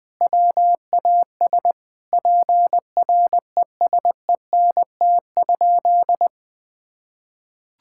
Morse code, heard as a keyed tone, about 20 wpm, 700 Hz. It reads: WAS PRESENT?
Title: Sentences from Top 1000 Words - Review Part 5 - 20wpm